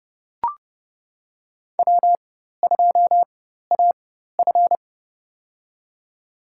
Morse code, heard as a keyed tone, about 31 wpm, spaced as for 14 wpm, 700 Hz. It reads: W2AF
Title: USA Call Signs - 4 Characters - 4x Character Spacing - 30wpm